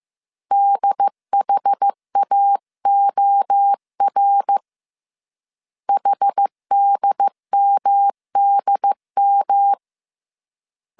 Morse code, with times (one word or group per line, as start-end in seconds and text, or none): 0.5-4.6 DHAOR
5.9-9.7 HDMDM